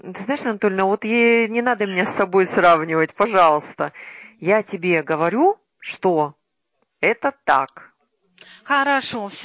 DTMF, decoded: none

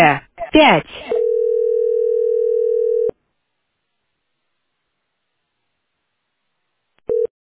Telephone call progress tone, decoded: ringback tone

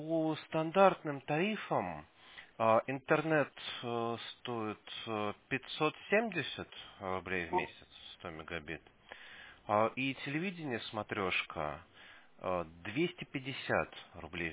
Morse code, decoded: none

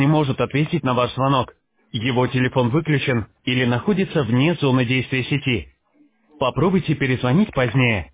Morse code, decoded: none

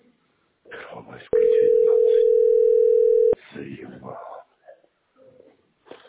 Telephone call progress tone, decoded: ringback tone

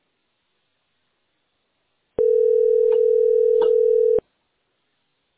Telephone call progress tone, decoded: ringback tone

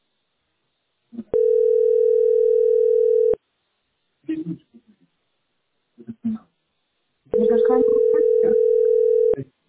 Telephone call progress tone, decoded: ringback tone

